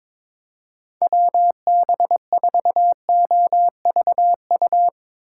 Morse code, heard as a keyed tone, about 22 wpm, 700 Hz. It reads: WB4OVU
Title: Speed Racing - USA 2x3 Call Signs 22wpm